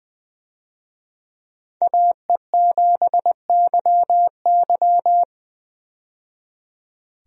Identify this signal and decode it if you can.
Morse code, heard as a keyed tone, about 20 wpm, 700 Hz. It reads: AE7YY
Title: USA Call Signs - 5 Characters - 20wpm